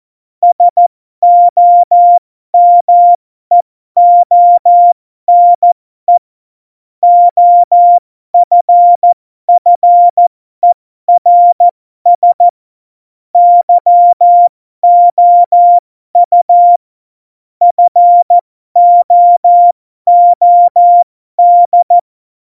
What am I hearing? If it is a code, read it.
Morse code, heard as a keyed tone, about 14 wpm, spaced as for 12 wpm, 700 Hz. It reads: SOMEONE OFFERS YOU FOOD
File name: Morsekurs / Witze